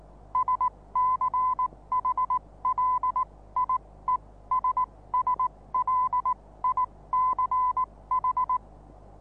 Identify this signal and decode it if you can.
Morse code, heard as a keyed tone, about 19 wpm, spaced as for 16 wpm, 1010 Hz. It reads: SCHLIESSLICH
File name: repos / Morsekurs / DJ4UF Morsen26